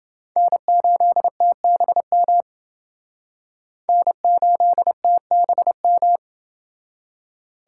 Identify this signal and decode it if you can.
Morse code, heard as a keyed tone, about 29 wpm, 700 Hz. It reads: N8TBM N8TBM